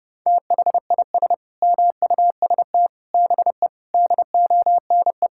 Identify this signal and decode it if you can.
Morse code, heard as a keyed tone, about 32 wpm, 700 Hz. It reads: THIS MUST BE DONE
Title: Sentences from Top 1600 Words 30wpm